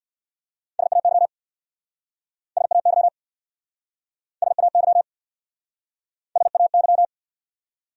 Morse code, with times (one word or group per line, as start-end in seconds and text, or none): 0.8-1.3 SAY
2.6-3.1 SAY
4.4-5.0 SAY
6.4-7.1 SAY